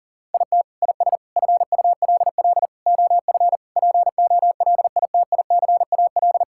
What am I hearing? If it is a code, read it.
Morse code, heard as a keyed tone, about 40 wpm, 700 Hz. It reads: IT IS FULL OF POLITICAL
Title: Sentences from Top 1000 Words - Review Part 5 - 40wpm